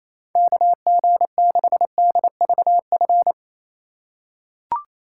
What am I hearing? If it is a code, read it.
Morse code, heard as a keyed tone, about 28 wpm, 700 Hz. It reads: KG6DVF
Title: USA Call Signs - 6 Characters - 2x Character Spacing - 28wpm